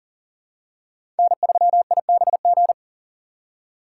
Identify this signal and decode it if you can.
Morse code, heard as a keyed tone, about 39 wpm, 700 Hz. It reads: N3IBG
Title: USA Call Signs - 5 Characters - 4x Character Spacing - 40wpm